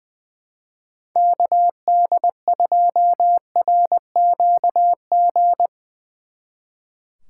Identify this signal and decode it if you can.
Morse code, heard as a keyed tone, about 22 wpm, 700 Hz. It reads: KD2RQG
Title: Speed Racing - USA 2x3 Call Signs 20wpm